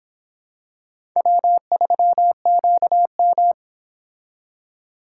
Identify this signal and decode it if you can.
Morse code, heard as a keyed tone, about 26 wpm, 700 Hz. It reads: W3QM